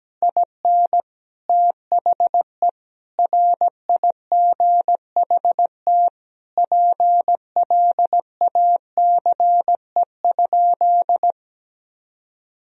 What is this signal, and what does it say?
Morse code, heard as a keyed tone, about 17 wpm, 700 Hz. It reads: IN THE RIGHT PLACE?
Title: Sentences from Top 1000 Words - Review Part 2 - 17wpm